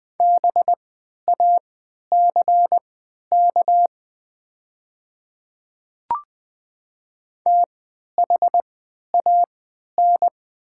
Morse code, heard as a keyed tone, about 20 wpm, spaced as for 11 wpm, 700 Hz. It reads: BACK THAN
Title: 3x Character Spacing - Top 300 Words - 20wpm